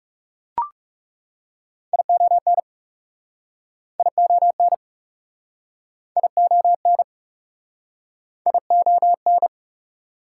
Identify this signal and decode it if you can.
Morse code, heard as a keyed tone, about 43 wpm, 690 Hz. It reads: ION ION ION ION